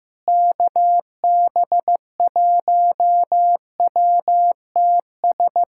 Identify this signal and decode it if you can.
Morse code, heard as a keyed tone, about 15 wpm, 700 Hz. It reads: KB1WTS
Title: USA Callsign DE Callsign - 6 Characters - 15wpm